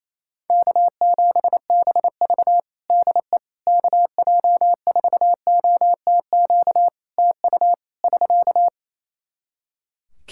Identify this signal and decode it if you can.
Morse code, heard as a keyed tone, about 27 wpm, 700 Hz. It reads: K7BV DE KJ4OTQ TU <SK>
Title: Practice QSO - Exchange 4 - Ending - 28wpm